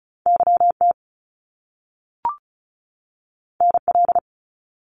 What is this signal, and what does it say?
Morse code, heard as a keyed tone, about 36 wpm, 700 Hz. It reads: YT NL
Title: Canadian Provinces and Territories 35wpm